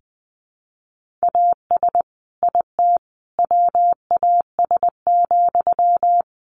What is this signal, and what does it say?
Morse code, heard as a keyed tone, about 20 wpm, 700 Hz. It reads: AS IT WAS,
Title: Sentences from Top 1400 Words 20wpm